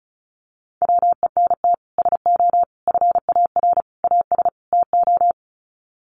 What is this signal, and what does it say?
Morse code, heard as a keyed tone, about 34 wpm, 700 Hz. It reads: WENT SO FAR AS TO
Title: Sets of 5 Words Encore 35wpm